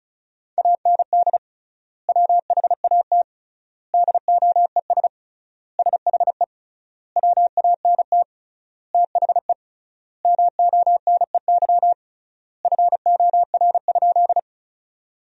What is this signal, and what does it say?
Morse code, heard as a keyed tone, about 34 wpm, 700 Hz. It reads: AND WHAT DOES SHE WANT THE MONEY FOR?
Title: 3x Word-Spacing - Sentences from Top 400 Words - 35wpm